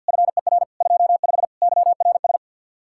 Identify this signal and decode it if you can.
Morse code, heard as a keyed tone, about 62 wpm, 700 Hz. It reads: FER 25 YRS
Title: Speed-Racing Common QSO Phrases - 50wpm